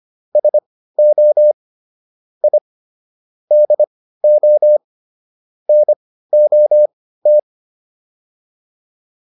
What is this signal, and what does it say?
Morse code, heard as a keyed tone, about 25 wpm, spaced as for 15 wpm, 600 Hz. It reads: SO I DO NOT
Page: intermediate-practice-files